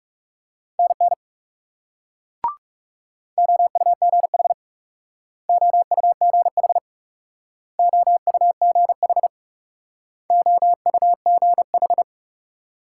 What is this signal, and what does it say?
Morse code, heard as a keyed tone, about 39 wpm, 700 Hz. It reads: NN OUGH OUGH OUGH OUGH